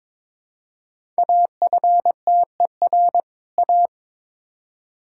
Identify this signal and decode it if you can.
Morse code, heard as a keyed tone, about 22 wpm, 700 Hz. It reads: AFTER A
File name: Build A Phrase - 2-Word Phrases - 22wpm